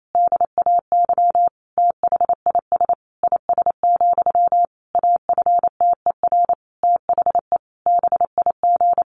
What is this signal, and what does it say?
Morse code, heard as a keyed tone, about 28 wpm, 700 Hz. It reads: DAY THIS IS, AFTER THE BIG
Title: Sentences from Top 1500 Words 28wpm